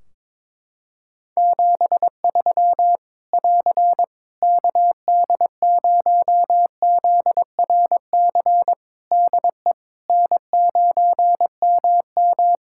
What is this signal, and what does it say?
Morse code, heard as a keyed tone, about 22 wpm, 700 Hz. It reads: 73 <AR> KD0ZRC DE N9MM
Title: Practice QSO - Exchange 4 - Ending - 22wpm